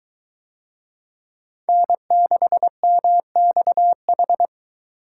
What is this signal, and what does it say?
Morse code, heard as a keyed tone, about 22 wpm, 700 Hz. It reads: N6MXH